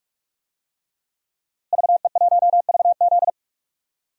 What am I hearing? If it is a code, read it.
Morse code, heard as a keyed tone, about 45 wpm, 700 Hz. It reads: VE1VZ
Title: DXCC-100 Call Signs - 5 Characters - Repeated 3x - 45wpm